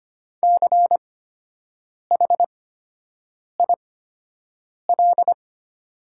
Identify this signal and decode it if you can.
Morse code, heard as a keyed tone, about 25 wpm, spaced as for 7 wpm, 700 Hz. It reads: CHIL
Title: 8x Character Spacing - Top 500 Words - 25wpm